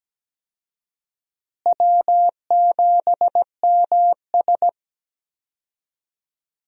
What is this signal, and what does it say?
Morse code, heard as a keyed tone, about 17 wpm, 700 Hz. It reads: W7MS
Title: USA Call Signs - 4 Characters - Repeated 3x - 17wpm